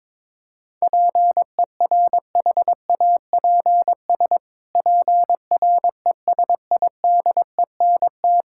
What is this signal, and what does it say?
Morse code, heard as a keyed tone, about 22 wpm, 700 Hz. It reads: PERHAPS PRESIDENT
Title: Pangram Challenge v2 - 22wpm